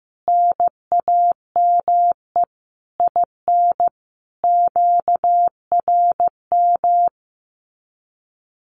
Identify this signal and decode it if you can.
Morse code, heard as a keyed tone, about 14 wpm, 700 Hz. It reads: NAME IN QRM
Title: Common QSO Phrases 15wpm